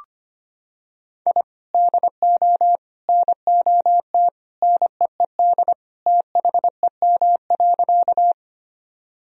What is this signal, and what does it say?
Morse code, heard as a keyed tone, about 25 wpm, 700 Hz. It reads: I DO NOT NEED THEM.
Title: Sentences from Top 1000 Words - Review Part 1 - 25wpm